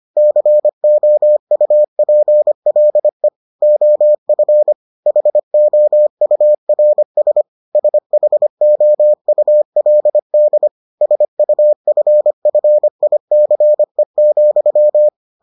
Morse code, heard as a keyed tone, about 25 wpm, 600 Hz. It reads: COUPLE OF HOURS SHOULD SUFFICE,